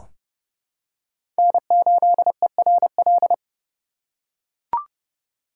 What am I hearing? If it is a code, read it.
Morse code, heard as a keyed tone, about 30 wpm, 700 Hz. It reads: N8ERL
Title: USA Call Signs - 5 Characters - 4x Character Spacing - 30wpm